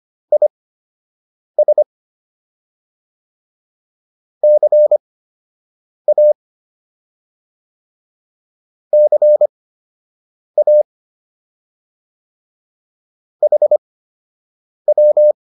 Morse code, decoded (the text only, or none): IS CA CA HW